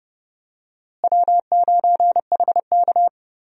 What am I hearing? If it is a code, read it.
Morse code, heard as a keyed tone, about 30 wpm, 700 Hz. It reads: W9HK
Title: USA Call Signs - 4 Characters - Repeated 2x - 30wpm